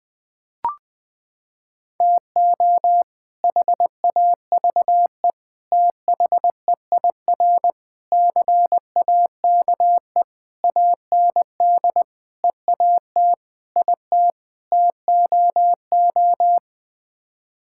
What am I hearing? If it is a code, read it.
Morse code, heard as a keyed tone, about 20 wpm, 700 Hz. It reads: TO HAVE THEIR CAKE AND EAT IT TOO
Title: Sets of 8 Words - 20wpm